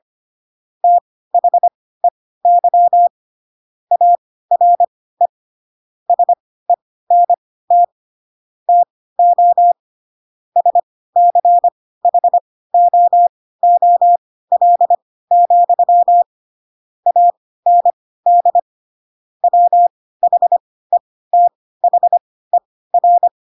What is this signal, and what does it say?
Morse code, heard as a keyed tone, about 25 wpm, spaced as for 16 wpm, 700 Hz. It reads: THEY ARE SENT TO SCHOOL, AND WHETHER
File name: Pinocchio / Pinocchio ch4 10 25 ik1hge